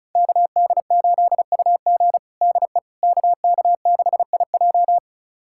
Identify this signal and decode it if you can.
Morse code, heard as a keyed tone, about 36 wpm, 700 Hz. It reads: KD8UG DE KK6IJ
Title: USA Callsign DE Callsign - 5 Characters - 35wpm